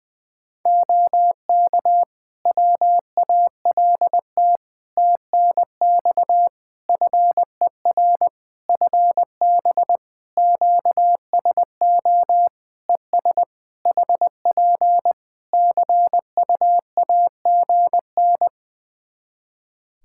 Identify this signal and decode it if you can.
Morse code, heard as a keyed tone, about 21 wpm, 700 Hz. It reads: OK WALT TNX FER FB QSO ES HP CUAGN
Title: Practice QSO - Exchange 4 - Ending - 20wpm